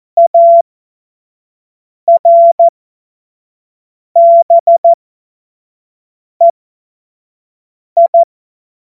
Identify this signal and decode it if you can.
Morse code, heard as a keyed tone, about 14 wpm, spaced as for 5 wpm, 680 Hz. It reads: ARBEI
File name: Morsekurs / Witze / Witz 24.Oktober 2023 25 F70